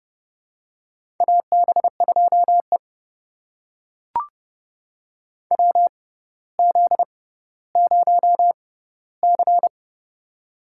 Morse code, heard as a keyed tone, about 30 wpm, spaced as for 15 wpm, 700 Hz. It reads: AB2E WZ0C
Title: USA Call Signs - 4 Characters - 6x Character Spacing - 30wpm